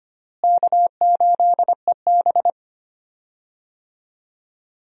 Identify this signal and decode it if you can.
Morse code, heard as a keyed tone, about 25 wpm, 700 Hz. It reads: K8EB
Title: USA Call Signs - 4 Characters - Repeated 2x - 25wpm